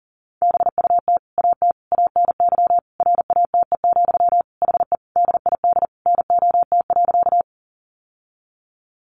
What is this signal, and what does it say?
Morse code, heard as a keyed tone, about 40 wpm, 700 Hz. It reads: BUT AT ANY RATE, HE DID NOT.
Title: Sentences from Top 700 Words 40wpm